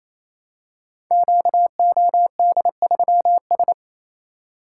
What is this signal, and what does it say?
Morse code, heard as a keyed tone, about 27 wpm, 700 Hz. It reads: QOD3S
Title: Code Groups - 5 Letter:Numbers - 28wpm